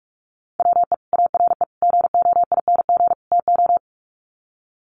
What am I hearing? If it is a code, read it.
Morse code, heard as a keyed tone, about 44 wpm, 700 Hz. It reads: WE ARE GOING TO
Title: Sets of 4 Words 45wpm